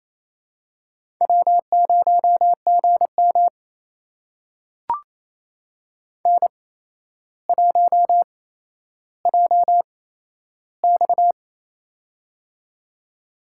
Morse code, heard as a keyed tone, about 29 wpm, spaced as for 12 wpm, 700 Hz. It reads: W0GM N1JX
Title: USA Call Signs - 4 Characters - 8x Character Spacing - 28wpm